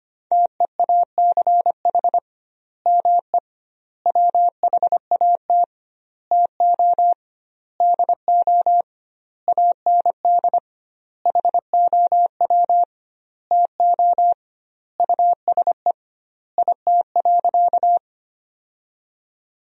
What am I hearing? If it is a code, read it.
Morse code, heard as a keyed tone, about 25 wpm, 700 Hz. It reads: TEACH ME WHAT TO DO AND HOW TO USE IT.
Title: 2x Word-Spacing - Sentences from Top 500 Words - 25wpm